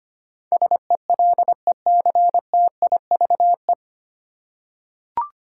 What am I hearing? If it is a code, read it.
Morse code, heard as a keyed tone, about 25 wpm, 700 Hz. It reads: SELECTIVE
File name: practice / 25wpm / 9 Letter Words - First 3 Letters Given - 25wpm